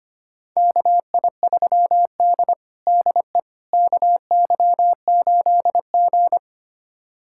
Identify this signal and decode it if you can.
Morse code, heard as a keyed tone, about 25 wpm, 700 Hz. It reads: KI3D DE KY8G
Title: USA Callsign DE Callsign - 4 Characters - 25wpm